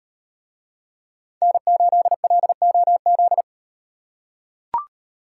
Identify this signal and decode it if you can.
Morse code, heard as a keyed tone, about 38 wpm, 700 Hz. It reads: N8LOZ